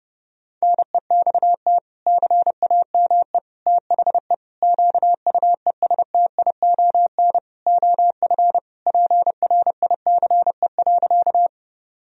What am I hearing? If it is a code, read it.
Morse code, heard as a keyed tone, about 30 wpm, 700 Hz. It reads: NEXT CAME THE QUESTION OF PRICE.
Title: Sentences from Top 1000 Words - Review Part 4 - 30wpm